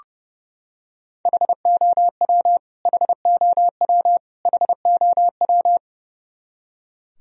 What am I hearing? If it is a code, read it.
Morse code, heard as a keyed tone, about 32 wpm, 700 Hz. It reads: HOW HOW HOW
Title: Top 100 Words Repeated 30wpm